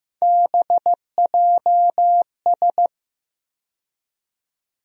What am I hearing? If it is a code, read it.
Morse code, heard as a keyed tone, about 16 wpm, 700 Hz. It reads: BJS
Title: USA Call Sign Postfixes - 3 Letters 15wpm